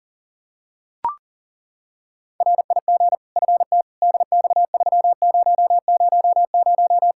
Morse code, heard as a keyed tone, about 40 wpm, 700 Hz. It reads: RIG FT DX3000
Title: Practice QSO - Exchange 2 - Info - 40wpm